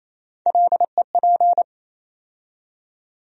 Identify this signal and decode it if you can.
Morse code, heard as a keyed tone, about 28 wpm, 700 Hz. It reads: LEP
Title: USA Call Sign Postfixes - 3 Letters 28wpm